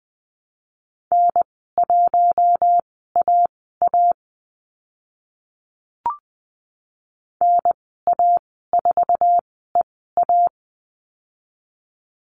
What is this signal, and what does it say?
Morse code, heard as a keyed tone, about 20 wpm, spaced as for 15 wpm, 700 Hz. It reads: N1AA NA4EA
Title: 076 Lesson 13g - Letter L - Call Signs 20wpm